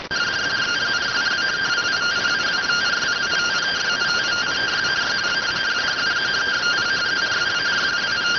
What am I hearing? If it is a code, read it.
Two-tone teleprinter signal, about 170 Hz shift.